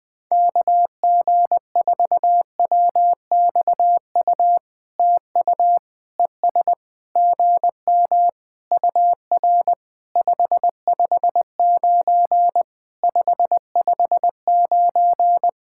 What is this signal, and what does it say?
Morse code, heard as a keyed tone, about 20 wpm, 700 Hz. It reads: KG4WXU TU ES GM UR 559 559
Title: POTA QSO Practice - 20wpm